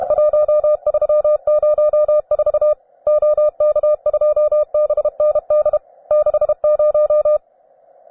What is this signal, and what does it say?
Morse code, heard as a keyed tone, about 32 wpm, 600 Hz. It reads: 1304 OK2BND 60
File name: cw advanced